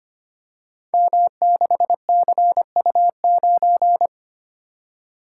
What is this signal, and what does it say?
Morse code, heard as a keyed tone, about 25 wpm, 700 Hz. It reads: M6CU9
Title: Code Groups - 5 Letter:Numbers - 25wpm